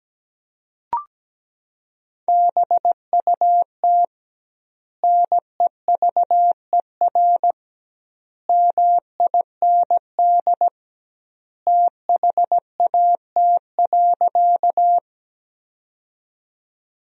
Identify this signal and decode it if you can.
Morse code, heard as a keyed tone, about 17 wpm, 700 Hz. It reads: BUT NEVER MIND THAT.
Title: 2x Word-Spacing - Sentences from Top 500 Words - 17wpm